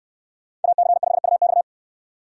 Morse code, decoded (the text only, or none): AC5UX